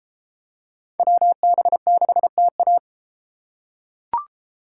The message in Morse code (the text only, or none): WB6TA